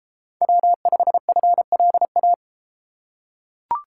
Morse code, heard as a keyed tone, about 31 wpm, 700 Hz. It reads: W5FLA